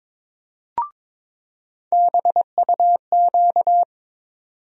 Morse code, heard as a keyed tone, about 22 wpm, 700 Hz. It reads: BUQ
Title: USA Call Sign Postfixes - 3 Letters 22wpm